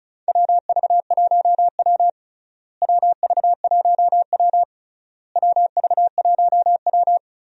Morse code, heard as a keyed tone, about 36 wpm, 700 Hz. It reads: WV1W WV1W WV1W
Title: USA Call Signs - 4 Characters - Repeated 3x - 35wpm